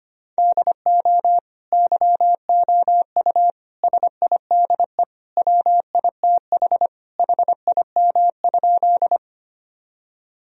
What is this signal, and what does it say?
Morse code, heard as a keyed tone, about 25 wpm, 700 Hz. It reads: DO YOU SIDE WITH HIM?
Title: Sentences from Top 1000 Words - Review Part 1 - 25wpm